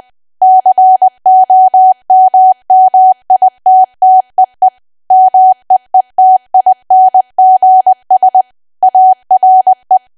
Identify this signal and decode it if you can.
Morse code, heard as a keyed tone, about 20 wpm, 750 Hz. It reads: COMMITTEE MEETINGS ARE
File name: cwa / W1AW-Level3 / 20WPM